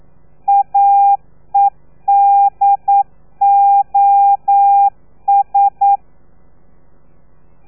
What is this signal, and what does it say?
Morse code, heard as a keyed tone, about 9 wpm, 800 Hz. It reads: AEDOS
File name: telegrafia / i2tfj / Lez3 40